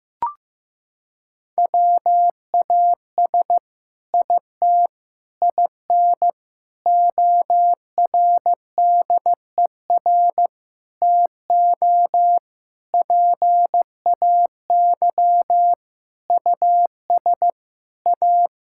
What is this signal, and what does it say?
Morse code, heard as a keyed tone, about 15 wpm, 700 Hz. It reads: WAS IT IN ORDER TO PAY US A
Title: Sentences from Top 1000 Words - Review Part 6 - 15wpm